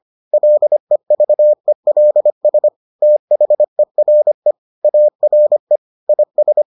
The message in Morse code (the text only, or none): LEVELS THERE ARE IS